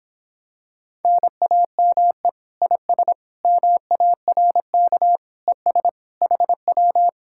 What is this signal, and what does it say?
Morse code, heard as a keyed tone, about 26 wpm, 700 Hz. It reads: NAME IS MARK ES HW